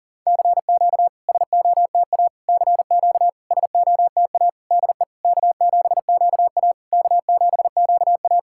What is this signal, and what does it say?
Morse code, heard as a keyed tone, about 40 wpm, 690 Hz. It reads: CQ SOTA CQ SOTA DE K7QA K7QA
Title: SOTA QSO Practice 40wpm